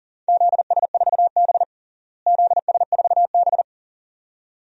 Morse code, heard as a keyed tone, about 40 wpm, 700 Hz. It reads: ZS4B ZS4B
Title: DXCC-100 Call Signs - 4 Characters - Repeated 2x - 40wpm